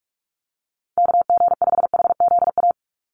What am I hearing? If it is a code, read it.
Morse code, heard as a keyed tone, about 44 wpm, 700 Hz. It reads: KG5HZA
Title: USA Call Signs 45wpm